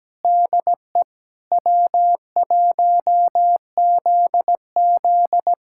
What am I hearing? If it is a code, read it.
Morse code, heard as a keyed tone, about 17 wpm, 700 Hz. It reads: DE W1ZZ
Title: USA Callsign DE Callsign - 4 Characters - 17wpm